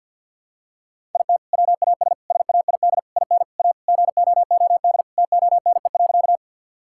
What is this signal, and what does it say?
Morse code, heard as a keyed tone, about 50 wpm, 700 Hz. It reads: IT WAS SAID IN A GOOD TONE.
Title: Sentences from Top 1600 Words 50wpm